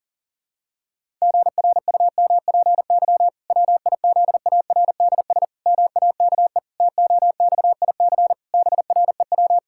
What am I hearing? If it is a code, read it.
Morse code, heard as a keyed tone, about 40 wpm, 700 Hz. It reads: GRUMPY WIZARDS MAKE TOXIC BREW
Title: Pangram Challenge v2 - 40wpm